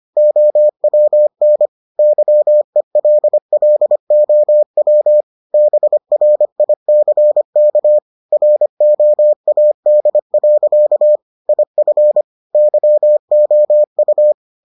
Morse code, heard as a keyed tone, about 25 wpm, 600 Hz. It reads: OWN YELLOW BRICK ROAD. IF YOU